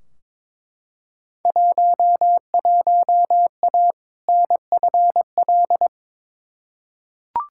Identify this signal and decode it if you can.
Morse code, heard as a keyed tone, about 22 wpm, 700 Hz. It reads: 11A NFL
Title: Field Day Exchanges Encore - 22wpm